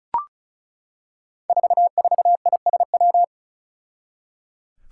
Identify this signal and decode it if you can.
Morse code, heard as a keyed tone, about 36 wpm, 700 Hz. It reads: 44ISW
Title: Five Challenging Letter-Number Combinations 35wpm